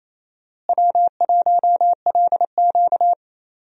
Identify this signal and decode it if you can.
Morse code, heard as a keyed tone, about 28 wpm, 700 Hz. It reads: W1LQ
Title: USA Call Signs - 4 Characters - Repeated 3x - 28wpm